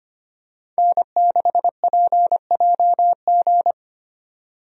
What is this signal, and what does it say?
Morse code, heard as a keyed tone, about 25 wpm, 700 Hz. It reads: N6PJG
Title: USA Call Signs - 5 Characters - 25wpm